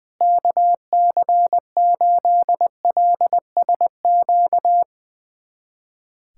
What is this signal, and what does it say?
Morse code, heard as a keyed tone, about 20 wpm, 700 Hz. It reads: KC8LSQ
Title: USA Callsign DE Callsign - 6 Characters - 20wpm